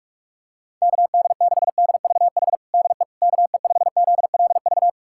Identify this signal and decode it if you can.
Morse code, heard as a keyed tone, about 44 wpm, 690 Hz. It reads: KD6DVH DE KE5ZLV
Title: USA Callsign DE Callsign 45wpm